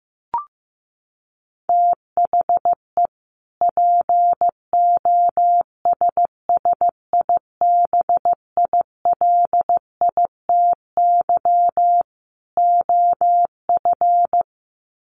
Morse code, heard as a keyed tone, about 15 wpm, 700 Hz. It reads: THE POSSIBILITY OF
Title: Sets of 3 Words 15wpm